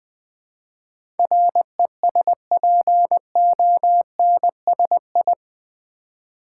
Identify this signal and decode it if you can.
Morse code, heard as a keyed tone, about 20 wpm, 700 Hz. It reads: RESPONSI
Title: Build A Word - 4 Letters At A Time - 11-Letter Words - 20wpm